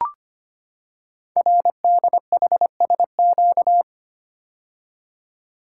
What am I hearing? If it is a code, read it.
Morse code, heard as a keyed tone, about 25 wpm, 700 Hz. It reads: RDHSQ
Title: Code Groups - 5 Letters - 25wpm